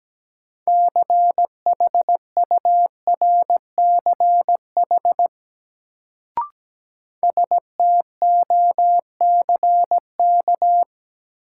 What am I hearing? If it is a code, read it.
Morse code, heard as a keyed tone, about 17 wpm, 700 Hz. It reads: CHURCH STOCK
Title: Sets of 5 Words - New Word Drill - 17wpm